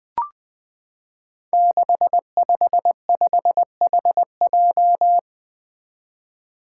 Morse code, heard as a keyed tone, about 20 wpm, 700 Hz. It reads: 655HJ
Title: Five Challenging Letter-Number Combinations 20wpm